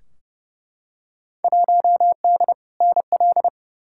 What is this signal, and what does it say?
Morse code, heard as a keyed tone, about 32 wpm, 700 Hz. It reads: 1D NL
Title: Field Day Exchanges Encore - 30wpm